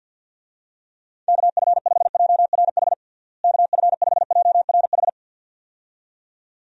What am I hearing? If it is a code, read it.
Morse code, heard as a keyed tone, about 49 wpm, 700 Hz. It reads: KF5JRH KF5JRH
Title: USA Call Signs - 6 Characters - Repeated 2x - 50wpm